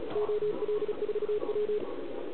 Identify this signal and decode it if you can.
Morse code, heard as a keyed tone, about 36 wpm, 410 Hz, with radio noise. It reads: OL4W